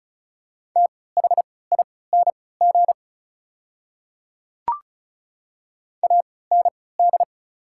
Morse code, heard as a keyed tone, about 36 wpm, spaced as for 20 wpm, 700 Hz. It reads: THING AND
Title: 3x Character Spacing - Top 300 Words - 35wpm